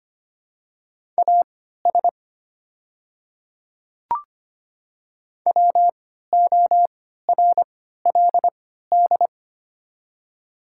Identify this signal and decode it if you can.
Morse code, heard as a keyed tone, about 25 wpm, spaced as for 14 wpm, 700 Hz. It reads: AS WORLD
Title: 3x Character Spacing - Top 200 Words - 25wpm